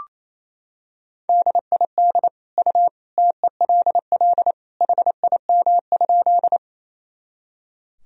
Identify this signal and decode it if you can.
Morse code, heard as a keyed tone, about 28 wpm, 700 Hz. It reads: DID U TELL HIM?
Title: Sentences from Top 200 Words and Abbreviations 28wpm